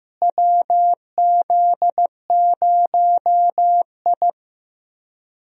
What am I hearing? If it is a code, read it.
Morse code, heard as a keyed tone, about 15 wpm, 700 Hz. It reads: WZ0I
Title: USA Call Signs - 4 Characters - Repeated 3x - 15wpm